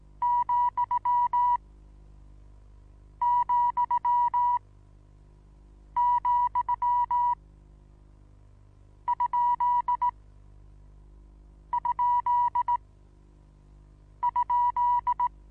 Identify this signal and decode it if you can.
Morse code, heard as a keyed tone, about 18 wpm, spaced as for 5 wpm, 970 Hz. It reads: ,,,???
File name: Onlinelehrgang / morsen / DJ4UF Morsen25